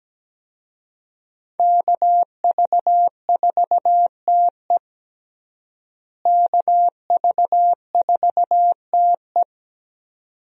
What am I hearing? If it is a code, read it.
Morse code, heard as a keyed tone, about 17 wpm, 700 Hz. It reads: KV4TE KV4TE
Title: USA Call Signs - 5 Characters - Repeated 2x - 17wpm